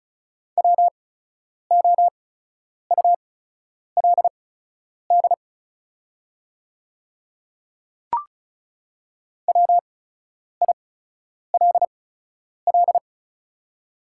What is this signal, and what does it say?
Morse code, heard as a keyed tone, about 36 wpm, spaced as for 10 wpm, 690 Hz. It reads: WOULD WILL
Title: 8x Character Spacing - Top 300 Words - 35wpm